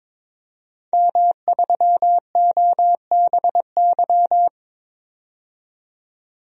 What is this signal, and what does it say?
Morse code, heard as a keyed tone, about 22 wpm, 700 Hz. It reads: M3OBY
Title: DXCC-100 Call Signs - 5 Characters - 22wpm